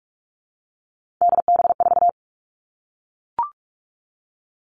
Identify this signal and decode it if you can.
Morse code, heard as a keyed tone, about 44 wpm, 700 Hz. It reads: DB4
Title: All Cut Numbers - 3-Digit 45wpm